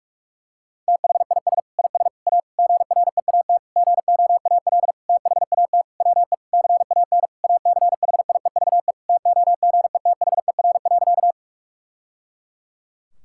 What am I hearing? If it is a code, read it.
Morse code, heard as a keyed tone, about 46 wpm, 700 Hz. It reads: THIS IS A GREAT GOAL THAT WE CAN ACHIEVE TOGETHER.
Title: Sentences from Top 1000 Words - Review Part 6 - 45wpm